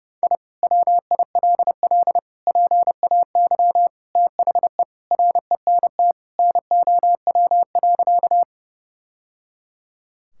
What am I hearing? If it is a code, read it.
Morse code, heard as a keyed tone, about 30 wpm, 700 Hz. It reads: I WILL PAY THE RENT NOW.
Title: Sentences from Top 1000 Words - Review Part 5 - 30wpm